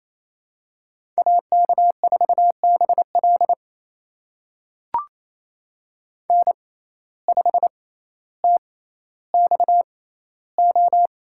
Morse code, heard as a keyed tone, about 28 wpm, spaced as for 14 wpm, 700 Hz. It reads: AK4BL N5TXO
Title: USA Call Signs - 5 Characters - 6x Character Spacing - 28wpm